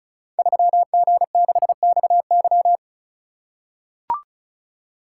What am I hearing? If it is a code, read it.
Morse code, heard as a keyed tone, about 33 wpm, 700 Hz. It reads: 3G6XY
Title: Code Groups - 5 Letter:Numbers - 35wpm